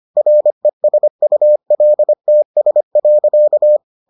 Morse code, heard as a keyed tone, about 25 wpm, 600 Hz. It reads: RESULTS.